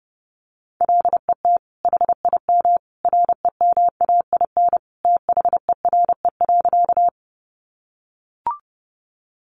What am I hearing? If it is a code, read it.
Morse code, heard as a keyed tone, about 28 wpm, 700 Hz. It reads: LET HIM REMAIN THERE.
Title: Sentences from Top 1200 Words 30wpm